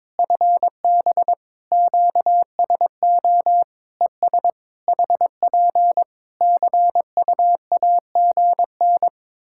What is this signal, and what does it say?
Morse code, heard as a keyed tone, about 22 wpm, 700 Hz. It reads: FB QSO ES HP CUAGN
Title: Practice QSO - Exchange 4 - Ending - 22wpm